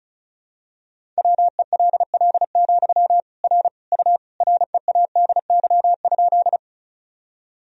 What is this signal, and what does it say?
Morse code, heard as a keyed tone, about 36 wpm, 700 Hz. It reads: WELL, R U READY?
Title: Sentences from Top 400 Words and Abbreviations 35wpm